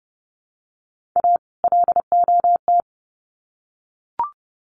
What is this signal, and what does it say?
Morse code, heard as a keyed tone, about 30 wpm, 700 Hz. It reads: A LOT
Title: Sets of 2 Words 30wpm